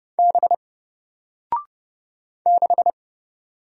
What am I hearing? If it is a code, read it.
Morse code, heard as a keyed tone, about 30 wpm, 700 Hz. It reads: B 6
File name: Single-Letter-Number-30wpm